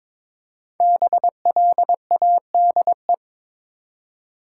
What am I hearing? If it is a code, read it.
Morse code, heard as a keyed tone, about 22 wpm, 700 Hz. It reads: BLADE
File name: 5 Letter Words - First 3 Letters Given - 22wpm